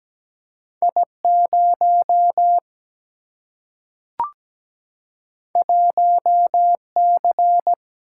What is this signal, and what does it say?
Morse code, heard as a keyed tone, about 17 wpm, 700 Hz. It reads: I 0 1 C
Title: Single-Letter Single-Number 17wpm